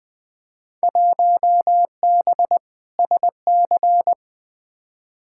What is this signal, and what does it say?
Morse code, heard as a keyed tone, about 20 wpm, 700 Hz. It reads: 1B SC
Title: Field Day Exchanges - 20wpm